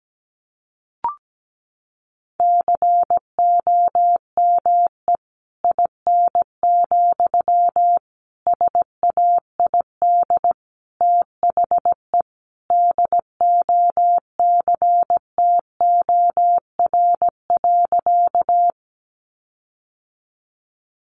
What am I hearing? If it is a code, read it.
Morse code, heard as a keyed tone, about 17 wpm, 700 Hz. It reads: COME IN, SAID THE DOCTOR.
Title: Sentences from Top 1500 Words 17wpm